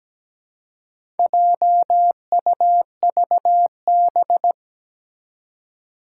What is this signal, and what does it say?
Morse code, heard as a keyed tone, about 17 wpm, 700 Hz. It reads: JUVB
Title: Four Challenging Letter-Number Combinations 17wpm